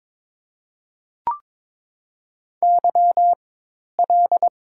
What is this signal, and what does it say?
Morse code, heard as a keyed tone, about 22 wpm, spaced as for 10 wpm, 700 Hz. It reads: YL
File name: Top 90 QSO Elements - 4x Character Spacing - 22wpm